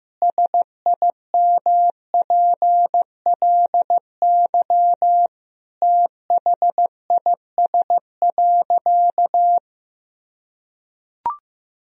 Morse code, heard as a keyed tone, about 16 wpm, 700 Hz. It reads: SIMPLY THIS.
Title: Sentences from Top 1000 Words - Review Part 4 - 15wpm